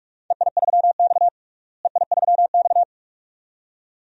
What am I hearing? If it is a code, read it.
Morse code, heard as a keyed tone, about 46 wpm, 700 Hz. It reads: EI3X EI3X
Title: DXCC-100 Call Signs - 4 Characters - Repeated 2x - 45wpm